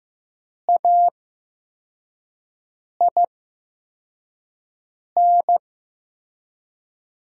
Morse code, heard as a keyed tone, about 15 wpm, spaced as for 4 wpm, 700 Hz. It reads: AIN
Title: 8x Character Spacing - Top 500 Words - 15wpm